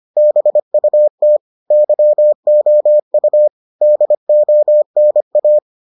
Morse code, heard as a keyed tone, about 25 wpm, 600 Hz. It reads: BUT YOU DONA